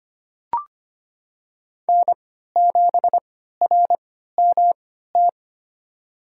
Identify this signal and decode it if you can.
Morse code, heard as a keyed tone, about 25 wpm, spaced as for 14 wpm, 700 Hz. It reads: N7RMT
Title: USA Call Signs - 3x Character Spacing - 25wpm